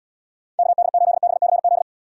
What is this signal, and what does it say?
Morse code, heard as a keyed tone, about 77 wpm, 700 Hz. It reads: KD8BCZ